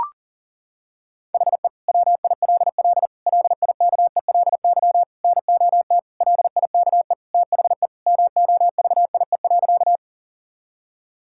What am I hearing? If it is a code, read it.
Morse code, heard as a keyed tone, about 40 wpm, 700 Hz. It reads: HE WILL LIKELY NOT LIKE THE MOVIE.
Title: Sentences from Top 1000 Words - Review Part 5 - 40wpm